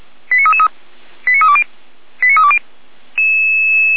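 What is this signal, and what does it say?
Five-tone selective call (ZVEI), heard as quick runs of five tones, tones of about 70 ms. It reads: 89333 89320 89320
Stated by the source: Five-tone ZVEI selective call received on SDR, fire brigade feuerwehr-orlamuende, code 89371.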